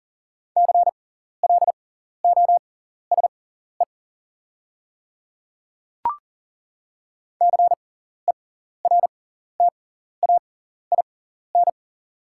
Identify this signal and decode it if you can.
Morse code, heard as a keyed tone, about 40 wpm, spaced as for 14 wpm, 700 Hz. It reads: CLOSE CERTAIN
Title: 6x Character Spacing - Top 500 Words - 40wpm